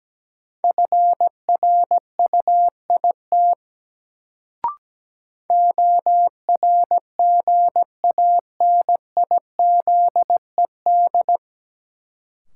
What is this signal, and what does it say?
Morse code, heard as a keyed tone, about 17 wpm, 700 Hz. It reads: FRUIT ORGANIZED
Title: Sentences from Top 1300 Words - New Word Drill - 17wpm